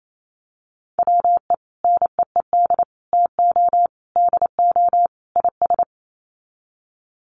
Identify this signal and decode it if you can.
Morse code, heard as a keyed tone, about 28 wpm, 700 Hz. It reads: WE NEED TO DO IS
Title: Sets of 5 Words Encore2 28wpm